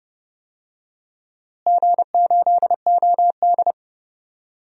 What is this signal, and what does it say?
Morse code, heard as a keyed tone, about 30 wpm, 700 Hz. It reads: G8OD
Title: DXCC-100 Call Signs - 4 Characters - Repeated 3x - 30wpm